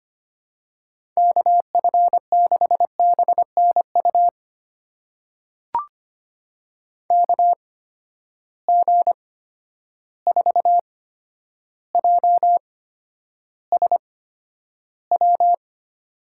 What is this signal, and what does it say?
Morse code, heard as a keyed tone, about 25 wpm, spaced as for 11 wpm, 700 Hz. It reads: KF6BNU KG4JSW
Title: USA Call Signs - 6 Characters - 8x Character Spacing - 25wpm